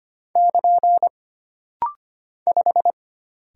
Morse code, heard as a keyed tone, about 25 wpm, 700 Hz. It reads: <KN> 5
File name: Single Character (Letter:Number:Punctuation:Prosign) - Rapid-Fire - 25wpm